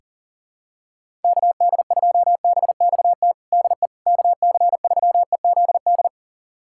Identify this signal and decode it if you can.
Morse code, heard as a keyed tone, about 40 wpm, 700 Hz. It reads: KD2BXT DE KC3EZD
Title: USA Callsign DE Callsign 40wpm